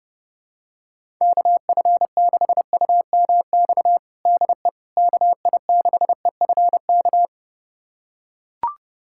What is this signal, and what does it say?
Morse code, heard as a keyed tone, about 30 wpm, 700 Hz. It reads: KF6UMX DE KI6EFK
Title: USA Callsign DE Callsign - 6 Characters - 30wpm